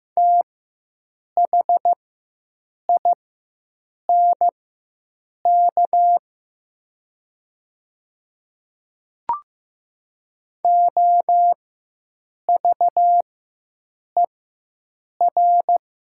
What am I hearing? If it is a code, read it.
Morse code, heard as a keyed tone, about 15 wpm, spaced as for 7 wpm, 700 Hz. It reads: THINK OVER
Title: 4x Character Spacing - Top 200 Words - 15wpm